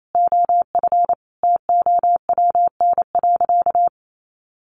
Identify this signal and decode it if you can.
Morse code, heard as a keyed tone, about 28 wpm, 700 Hz. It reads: OF TOWN.
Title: Sentences from Top 1500 Words 28wpm